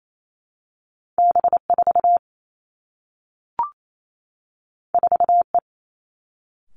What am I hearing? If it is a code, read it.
Morse code, heard as a keyed tone, about 28 wpm, 700 Hz. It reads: B 4 4 E
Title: Two Challenging Letter-Number Combinations 28wpm